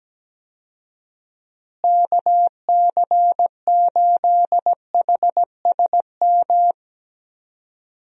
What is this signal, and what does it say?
Morse code, heard as a keyed tone, about 17 wpm, 700 Hz. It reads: KC8HSM
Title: USA Call Signs - 6 Characters - Repeated 2x - 17wpm